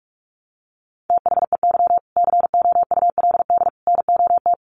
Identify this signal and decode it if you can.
Morse code, heard as a keyed tone, about 44 wpm, 700 Hz. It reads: THEY COULD NOT
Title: Sentences from Top 1100 Words 45wpm